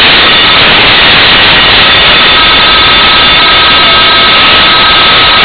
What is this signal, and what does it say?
PSK31, about 31.25 baud.